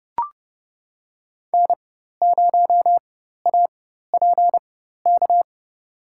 Morse code, heard as a keyed tone, about 30 wpm, spaced as for 14 wpm, 700 Hz. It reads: N0APK
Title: USA Call Signs - 4x Character Spacing - 30wpm